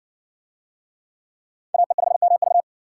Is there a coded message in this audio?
Yes: Morse code from 1.7 to 2.6 s, heard as a keyed tone, about 54 wpm, 700 Hz. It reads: AE5MV